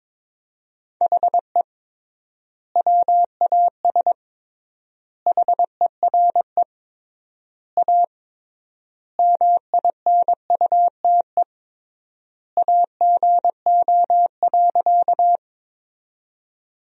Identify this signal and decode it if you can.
Morse code, heard as a keyed tone, about 22 wpm, 700 Hz. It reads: HE WAS HERE A MINUTE AGO.
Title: 3x Word-Spacing - Sentences from Top 500 Words - 22wpm